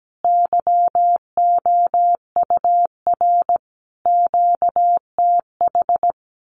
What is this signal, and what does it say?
Morse code, heard as a keyed tone, about 17 wpm, 700 Hz. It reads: YOUR QTH
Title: Common QSO Phrases 17wpm